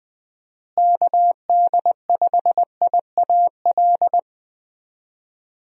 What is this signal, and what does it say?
Morse code, heard as a keyed tone, about 20 wpm, 700 Hz. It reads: KD5IAL
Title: USA Call Signs - 6 Characters - Repeated 2x - 20wpm